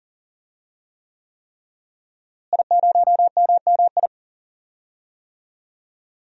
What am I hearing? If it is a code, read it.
Morse code, heard as a keyed tone, about 40 wpm, 700 Hz. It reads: I0MMI
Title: DXCC-100 Call Signs - 5 Characters - Repeated 3x - 40wpm